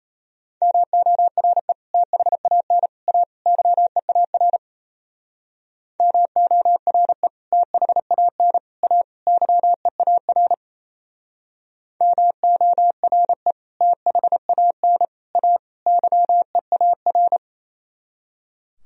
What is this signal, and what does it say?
Morse code, heard as a keyed tone, about 34 wpm, 690 Hz. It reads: MORE THAN A YEAR MORE THAN A YEAR MORE THAN A YEAR
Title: Speed-Racing - Sets of 4 Words 28wpm